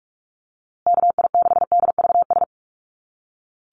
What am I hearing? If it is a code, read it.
Morse code, heard as a keyed tone, about 44 wpm, 700 Hz. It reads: KI6DVS
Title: USA Call Signs Repeated 2x 45wpm